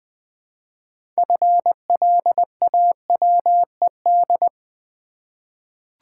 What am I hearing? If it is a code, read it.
Morse code, heard as a keyed tone, about 20 wpm, 700 Hz. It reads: FLAWED
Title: 6 Letter Words - First 3 Letters Given - 20wpm